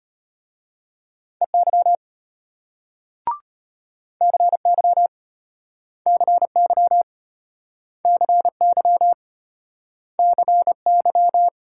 Morse code, heard as a keyed tone, about 34 wpm, 700 Hz. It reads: EY CY CY CY CY